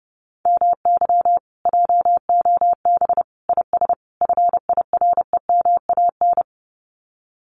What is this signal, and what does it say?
Morse code, heard as a keyed tone, about 30 wpm, 700 Hz. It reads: MY JOB IS FIREMAN
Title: Common QSO Phrases Encore 30wpm